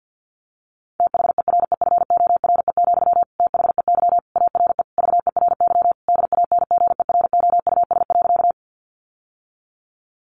Morse code, heard as a keyed tone, about 50 wpm, 700 Hz. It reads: THEREFORE, THEY ARE VERY DANGEROUS.
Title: Sentences from Top 1200 Words 50wpm